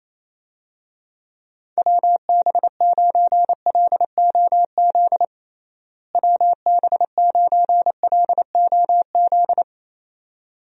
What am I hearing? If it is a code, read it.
Morse code, heard as a keyed tone, about 27 wpm, 700 Hz. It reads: WB9LOZ WB9LOZ
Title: USA Call Signs - 6 Characters - Repeated 2x - 28wpm